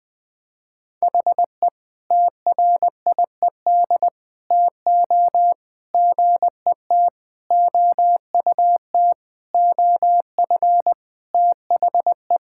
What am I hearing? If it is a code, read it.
Morse code, heard as a keyed tone, about 19 wpm, 700 Hz. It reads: HE TRIED TO GET OUT OF THE
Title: Sentences with 5-Word Phrases - 20wpm